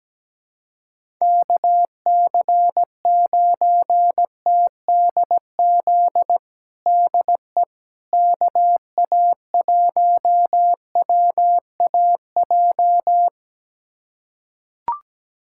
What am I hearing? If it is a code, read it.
Morse code, heard as a keyed tone, about 17 wpm, 700 Hz. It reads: KC9TDZ DE KA1WAJ
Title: USA Callsign DE Callsign - 6 Characters - 17wpm